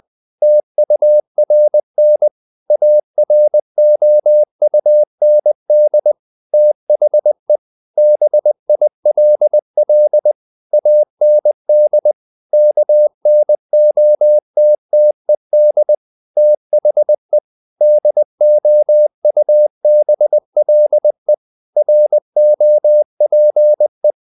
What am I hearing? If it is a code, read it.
Morse code, heard as a keyed tone, about 20 wpm, 600 Hz. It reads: TURN AROUND THE BILL AND KNOTTED THE DOUBLE ROPE
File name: the old man and the sea part 2 20wpm